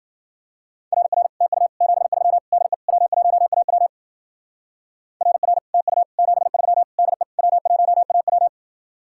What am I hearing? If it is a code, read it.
Morse code, heard as a keyed tone, about 57 wpm, 700 Hz. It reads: RR TU 73 DE W1AW RR TU 73 DE W1AW